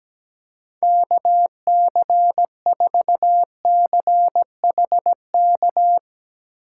Morse code, heard as a keyed tone, about 17 wpm, 700 Hz. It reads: KC4CHK
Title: USA Call Signs - 6 Characters - 2x Character Spacing - 17wpm